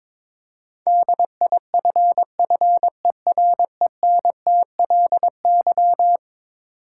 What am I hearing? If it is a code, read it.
Morse code, heard as a keyed tone, about 22 wpm, 700 Hz. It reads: DIFFERENTLY
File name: practice / 22wpm / Eleven Letter Words - 22wpm